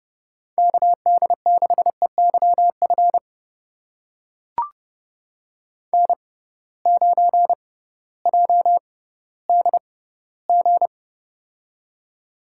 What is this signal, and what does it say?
Morse code, heard as a keyed tone, about 30 wpm, spaced as for 16 wpm, 700 Hz. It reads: KD6EYF N9JDG
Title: USA Call Signs - 6x Character Spacing - 30wpm